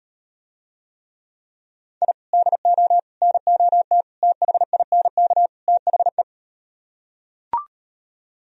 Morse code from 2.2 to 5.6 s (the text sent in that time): DO NOT THINK